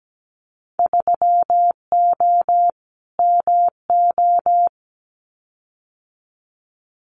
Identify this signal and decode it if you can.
Morse code, heard as a keyed tone, about 17 wpm, 700 Hz. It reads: 3O MO
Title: Winter Field Day Exchanges Encore 17wpm